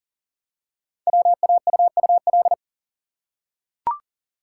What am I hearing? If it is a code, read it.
Morse code, heard as a keyed tone, about 40 wpm, 700 Hz. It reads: WAUUL
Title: Code Groups - 5 Letters - 40wpm